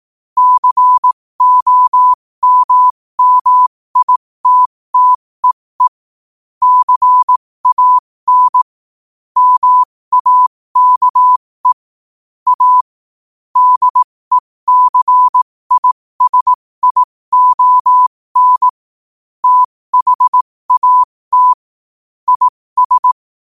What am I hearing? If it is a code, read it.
Morse code, heard as a keyed tone, about 18 wpm, spaced as for 15 wpm, 1000 Hz. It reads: COMMITTEE CAN MAKE A DECISION THAT IS
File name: 15wpm